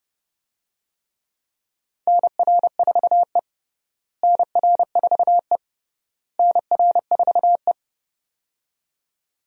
Morse code, heard as a keyed tone, about 32 wpm, 700 Hz. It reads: NR4E NR4E NR4E
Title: USA Call Signs - 4 Characters - Repeated 3x - 30wpm